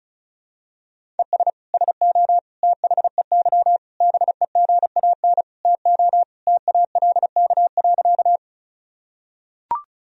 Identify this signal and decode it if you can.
Morse code, heard as a keyed tone, about 36 wpm, 700 Hz. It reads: ES SO THEY BEGAN TO TALK.
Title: Sentences from Top 400 Words and Abbreviations 35wpm